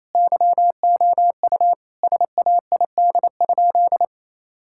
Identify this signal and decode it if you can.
Morse code, heard as a keyed tone, about 28 wpm, 700 Hz. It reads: YOU SAID?
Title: Sentences from Top 1600 Words 28wpm